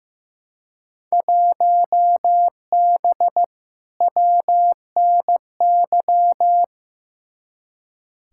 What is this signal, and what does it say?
Morse code, heard as a keyed tone, about 15 wpm, 700 Hz. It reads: 1B WNY
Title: Field Day Exchanges Encore - 15wpm